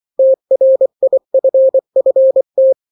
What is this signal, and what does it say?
Morse code, heard as a keyed tone, about 24 wpm, 530 Hz. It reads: TRIFFT
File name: daily / 23wpm